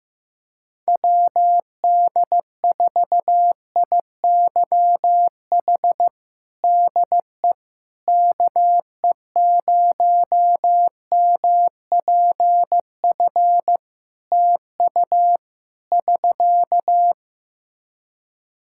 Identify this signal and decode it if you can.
Morse code, heard as a keyed tone, about 15 wpm, 700 Hz. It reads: WD4IYH DE KE0MPF TU <SK>
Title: Practice QSO - Exchange 4 - Ending - 15wpm